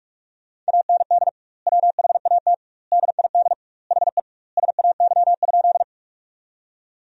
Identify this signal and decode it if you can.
Morse code, heard as a keyed tone, about 46 wpm, 700 Hz. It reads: AND WHAT DID HE SAY?
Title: 2x Word-Spacing - Sentences from Top 200 Words - 45wpm